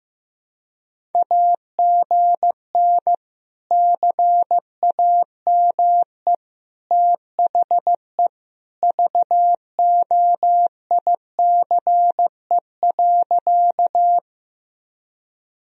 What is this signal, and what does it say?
Morse code, heard as a keyed tone, about 15 wpm, 700 Hz. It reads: AGN CAME THE VOICE.
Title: Sentences from Top 500 Words and Abbreviations 15wpm